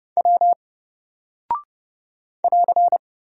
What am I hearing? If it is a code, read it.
Morse code, heard as a keyed tone, about 30 wpm, 700 Hz. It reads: W <AR>
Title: Single Character (Letter:Number:Punctuation:Prosign) Alt 30wpm